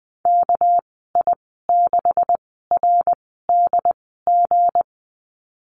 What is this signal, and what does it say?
Morse code, heard as a keyed tone, about 19 wpm, spaced as for 14 wpm, 700 Hz. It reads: KI6RDG
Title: 200 Lesson 32g - Letter k - Call Signs 20wpm